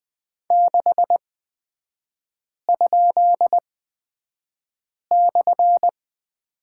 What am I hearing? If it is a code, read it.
Morse code, heard as a keyed tone, about 20 wpm, spaced as for 5 wpm, 700 Hz. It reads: 6?/
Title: 255 Lesson 41e - Period - Three Characters 20wpm